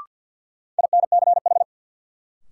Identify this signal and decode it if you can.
Morse code, heard as a keyed tone, about 51 wpm, 700 Hz. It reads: INCH
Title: Sentences from Top 1000 Words - Review Part 3 - New Word Drill - 50wpm